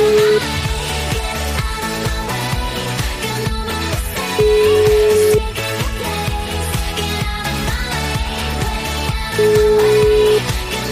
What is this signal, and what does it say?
Ringback tone (Europe, 425 Hz), with noise.